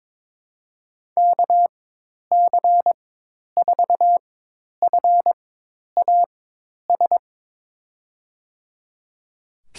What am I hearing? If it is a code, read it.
Morse code, heard as a keyed tone, about 22 wpm, spaced as for 10 wpm, 700 Hz. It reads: KC4FAS
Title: USA Call Signs - 4x Character Spacing - 22wpm